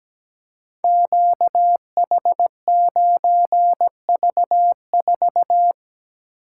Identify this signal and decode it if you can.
Morse code, heard as a keyed tone, about 17 wpm, 700 Hz. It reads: QH9V4
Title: Code Groups - 5 Letter:Numbers - 17wpm